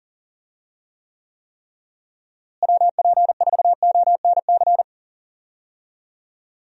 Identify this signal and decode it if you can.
Morse code, heard as a keyed tone, about 40 wpm, 700 Hz. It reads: WP4ONC
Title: USA Call Signs - 6 Characters - Repeated 3x - 40wpm